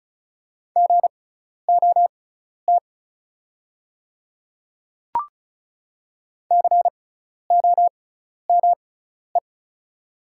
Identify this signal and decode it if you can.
Morse code, heard as a keyed tone, about 34 wpm, spaced as for 12 wpm, 690 Hz. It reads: GOT COME
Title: 6x Character Spacing - Top 300 Words - 35wpm